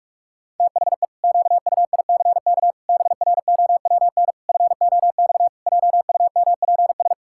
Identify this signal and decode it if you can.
Morse code, heard as a keyed tone, about 42 wpm, 700 Hz. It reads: THE QUICK BROWN FOX JUMPS
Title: Pangram Challenge v2 - 45wpm